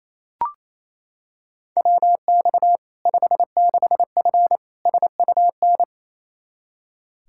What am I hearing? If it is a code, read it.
Morse code, heard as a keyed tone, about 27 wpm, 700 Hz. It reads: WX 56F SUN
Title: Practice QSO - Exchange 2 - Info - 28wpm